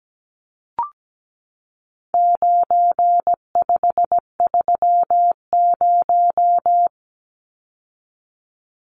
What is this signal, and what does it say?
Morse code, heard as a keyed tone, about 17 wpm, 700 Hz. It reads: 9530
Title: Numbers - 4-Digit 17wpm